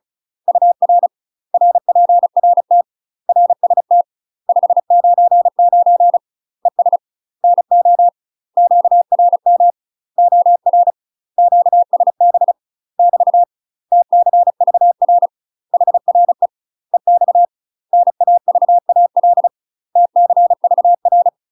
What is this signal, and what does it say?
Morse code, heard as a keyed tone, about 34 wpm, 700 Hz. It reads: UR RPRT RST 599 ES NO QRM OR QSB <BT> TCVR HRE EX NAVAL TCVR
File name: UK MASTER QSOs TRAINING / 35WPM / UK MASTER QSOs 0000